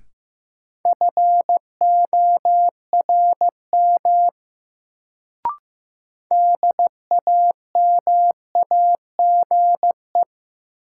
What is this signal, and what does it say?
Morse code, heard as a keyed tone, about 16 wpm, 700 Hz. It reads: FORM DAMAGE
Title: Sentences from Top 1100 Words - New Word Drill Alt - 15wpm